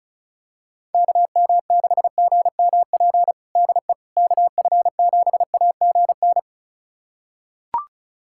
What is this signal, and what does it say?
Morse code, heard as a keyed tone, about 36 wpm, 700 Hz. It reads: KM6GMP DE KF7AGN
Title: USA Callsign DE Callsign - 6 Characters - 35wpm